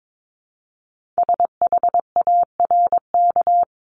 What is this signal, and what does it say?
Morse code, heard as a keyed tone, about 22 wpm, 700 Hz. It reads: SHARK
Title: Multi-letter Phonemes 22wpm